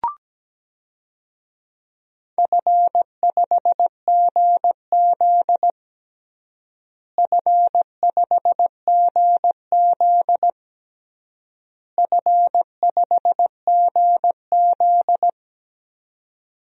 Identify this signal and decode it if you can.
Morse code, heard as a keyed tone, about 17 wpm, 700 Hz. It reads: F5GZ F5GZ F5GZ
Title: DXCC-100 Call Signs - 4 Characters - Repeated 3x - 17wpm